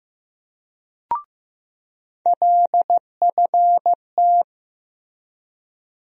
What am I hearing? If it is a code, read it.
Morse code, heard as a keyed tone, about 15 wpm, 700 Hz. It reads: LFT
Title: USA Call Sign Postfixes - 3 Letters 15wpm